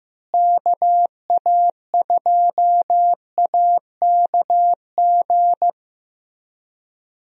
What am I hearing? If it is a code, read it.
Morse code, heard as a keyed tone, about 15 wpm, 700 Hz. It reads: KA2AKG
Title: USA Call Signs - 6 Characters - Repeated 3x - 15wpm